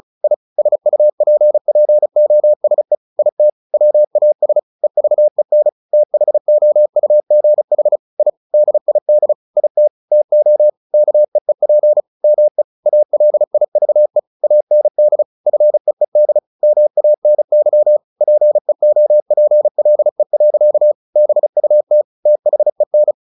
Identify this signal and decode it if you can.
Morse code, heard as a keyed tone, about 34 wpm, 600 Hz. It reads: I SUPPOSE IT WAS EVEN THOUGH I DID IT TO KEEP ME ALIVE AND FEED MANY PEOPLE. BUT THEN
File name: the old man and the sea 2 35wpm